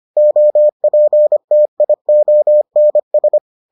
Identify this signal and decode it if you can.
Morse code, heard as a keyed tone, about 25 wpm, 600 Hz. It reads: OPTIONS